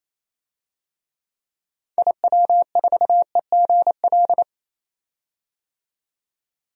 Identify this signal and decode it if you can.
Morse code, heard as a keyed tone, about 28 wpm, 700 Hz. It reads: IW4EGL
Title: DXCC-100 Call Signs - 6 Characters - 28wpm